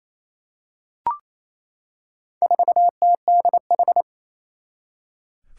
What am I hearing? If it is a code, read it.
Morse code, heard as a keyed tone, about 28 wpm, 700 Hz. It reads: 4TDH
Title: Four Challenging Letter-Number Combinations 28wpm